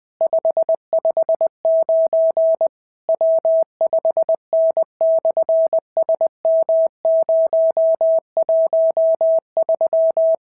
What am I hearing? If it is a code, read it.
Morse code, heard as a keyed tone, about 20 wpm, 690 Hz. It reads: 559 W5N/SM013
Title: SOTA QSO Practice 20wpm